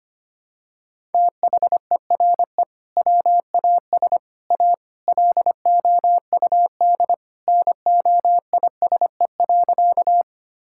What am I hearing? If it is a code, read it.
Morse code, heard as a keyed tone, about 25 wpm, 700 Hz. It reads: THERE WAS A LOUD NOISE.
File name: practice / 25wpm / Sentences with 3-Word Phrases - 25wpm